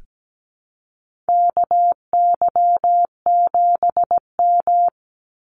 Morse code, heard as a keyed tone, about 17 wpm, 700 Hz. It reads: KY7M
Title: Call Signs Repeated 3x - 17wpm